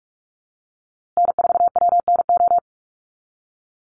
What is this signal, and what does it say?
Morse code, heard as a keyed tone, about 44 wpm, 700 Hz. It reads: N4WNO
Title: USA Call Signs Repeated 2x 45wpm